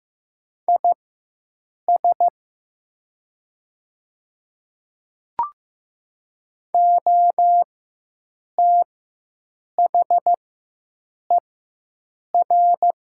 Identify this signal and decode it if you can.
Morse code, heard as a keyed tone, about 15 wpm, spaced as for 7 wpm, 700 Hz. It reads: IS OTHER
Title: 4x Character Spacing - Top 100 Words - 15wpm